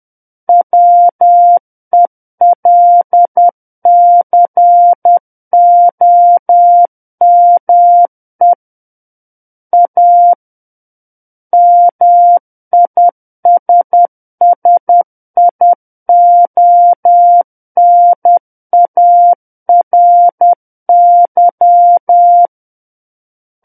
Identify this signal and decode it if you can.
Morse code, heard as a keyed tone, about 10 wpm, 700 Hz. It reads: WELCOME A MISSIONARY
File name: Book One / 01-Chapter 1 - 10 WPM